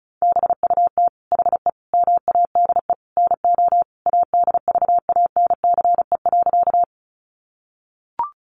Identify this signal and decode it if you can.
Morse code, heard as a keyed tone, about 34 wpm, 700 Hz. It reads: BUT HE MADE NO ADVANCE.
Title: Sentences from Top 1400 Words 35wpm